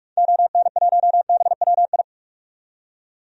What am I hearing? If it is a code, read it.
Morse code, heard as a keyed tone, about 46 wpm, 700 Hz. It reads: ON1BWI
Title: DXCC-100 Call Signs - 6 Characters - 45wpm